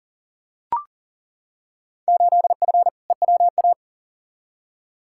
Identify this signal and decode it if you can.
Morse code, heard as a keyed tone, about 40 wpm, 700 Hz. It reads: 8F EWA
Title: Field Day Exchanges Encore - 40wpm